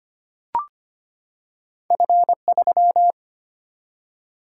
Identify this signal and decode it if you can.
Morse code, heard as a keyed tone, about 25 wpm, 700 Hz. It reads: F 3
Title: Single-Letter Single-Number 25wpm